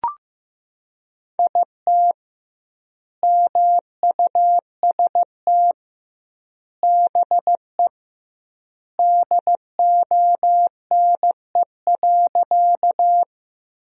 Morse code, heard as a keyed tone, about 16 wpm, 700 Hz. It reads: IT MUST BE DONE.
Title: 2x Word-Spacing - Sentences from Top 500 Words - 15wpm